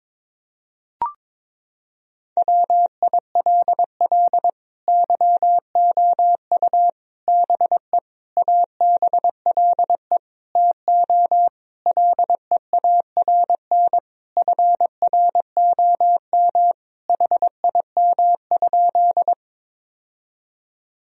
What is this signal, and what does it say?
Morse code, heard as a keyed tone, about 22 wpm, 700 Hz. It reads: WILL YOU BE ABLE TO LEARN FROM HIM?
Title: Sentences from Top 1000 Words - Review Part 2 - 22wpm